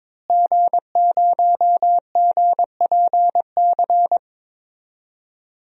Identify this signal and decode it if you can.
Morse code, heard as a keyed tone, about 22 wpm, 700 Hz. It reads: G0GPC
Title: DXCC-100 Call Signs - 5 Characters - Repeated 3x - 22wpm